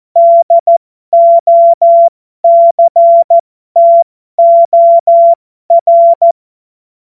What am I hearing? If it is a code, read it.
Morse code, heard as a keyed tone, about 14 wpm, spaced as for 12 wpm, 680 Hz. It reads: DOCTOR
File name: Morsekurs / Witze